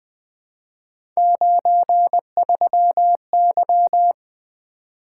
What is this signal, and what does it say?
Morse code, heard as a keyed tone, about 20 wpm, 700 Hz. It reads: 93Y
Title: Three Letter-Number Combinations - 20wpm